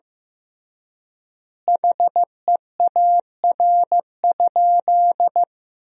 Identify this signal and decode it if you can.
Morse code, heard as a keyed tone, about 15 wpm, 700 Hz. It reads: HEAR?
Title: 3x Word-Spacing - Sentences from Top 300 Words - 15wpm